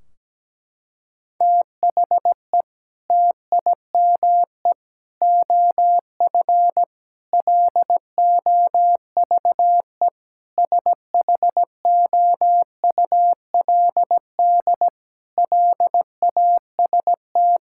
Morse code, heard as a keyed tone, about 17 wpm, 700 Hz. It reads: THE TIME OF LOVE SHOULD LAST
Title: Sentences from Top 1600 Words 17wpm